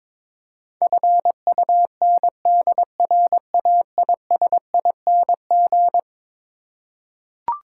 Morse code, heard as a keyed tone, about 22 wpm, 700 Hz. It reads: FUNDRAISING
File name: practice / 22wpm / Eleven Letter Words - 22wpm